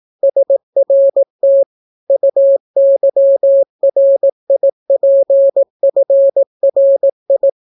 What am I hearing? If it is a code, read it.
Morse code, heard as a keyed tone, about 18 wpm, 550 Hz. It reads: SRT UYRIPFRI